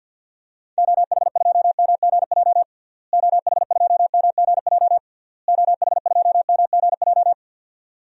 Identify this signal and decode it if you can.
Morse code, heard as a keyed tone, about 49 wpm, 700 Hz. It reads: OH2MGJ OH2MGJ OH2MGJ
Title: DXCC-100 Call Signs - 6 Characters - Repeated 3x - 50wpm